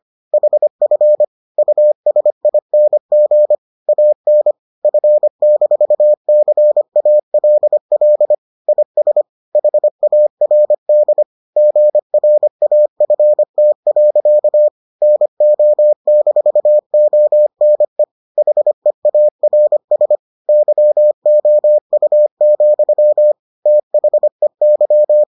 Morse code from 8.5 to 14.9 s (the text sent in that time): IS HARD GRAFT.